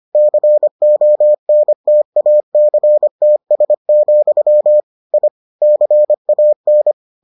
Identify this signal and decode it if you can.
Morse code, heard as a keyed tone, about 25 wpm, 600 Hz. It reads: CONTACTS, I CAN